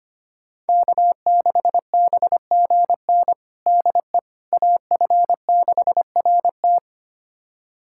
Morse code, heard as a keyed tone, about 25 wpm, 700 Hz. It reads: K6BGN DE AF6RT
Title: USA Callsign DE Callsign - 5 Characters - 25wpm